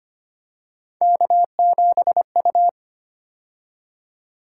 Morse code, heard as a keyed tone, about 25 wpm, 700 Hz. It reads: K7U
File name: Three Letter-Number Combinations - 25wpm